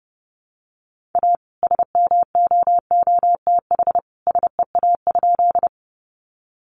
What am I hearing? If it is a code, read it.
Morse code, heard as a keyed tone, about 30 wpm, 700 Hz. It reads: A SMOOTH SEA?
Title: Sentences from Top 1400 Words 30wpm